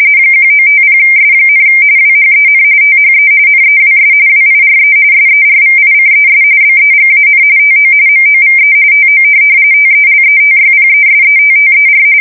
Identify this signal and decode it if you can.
RTTY, about 45.45 baud. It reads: 1950S, WHEN THE NEWS WIRE SERVICES BEGAN TO MOVE TO TELETYPESETTER
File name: fsk170 / teletype-corp